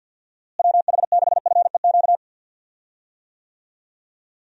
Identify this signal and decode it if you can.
Morse code, heard as a keyed tone, about 51 wpm, 700 Hz. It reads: WH6FEQ
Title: USA Call Signs - 6 Characters - Repeated 2x - 50wpm